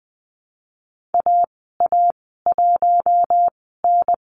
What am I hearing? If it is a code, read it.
Morse code, heard as a keyed tone, about 20 wpm, spaced as for 14 wpm, 700 Hz. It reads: AA1N
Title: 069 Lesson 12g - Letter d - Call Signs 20wpm